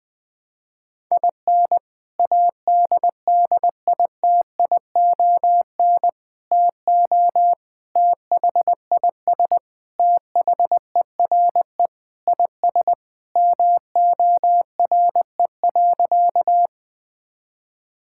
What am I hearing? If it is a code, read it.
Morse code, heard as a keyed tone, about 20 wpm, 700 Hz. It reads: IN ADDITION TO THIS THERE IS MORE.
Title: Sentences with 3-Word Phrases - 20wpm